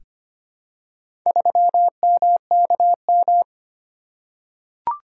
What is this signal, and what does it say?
Morse code, heard as a keyed tone, about 25 wpm, 700 Hz. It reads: 3MKM
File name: practice / 25wpm / Four Letter-Number Combinations - 25wpm